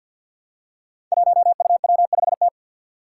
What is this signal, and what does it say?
Morse code, heard as a keyed tone, about 48 wpm, 700 Hz. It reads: 1UW5T
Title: Five Challenging Letter-Number Combinations 50wpm